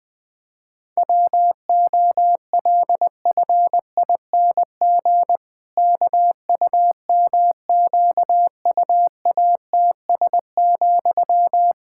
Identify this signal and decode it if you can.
Morse code, heard as a keyed tone, about 21 wpm, 700 Hz. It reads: WOLFING KUMQUATS,
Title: Pangram Challenge v2 - 20wpm